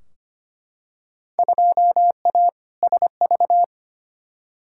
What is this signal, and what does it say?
Morse code, heard as a keyed tone, about 25 wpm, 700 Hz. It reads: 2A SV
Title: Field Day Exchanges - 25wpm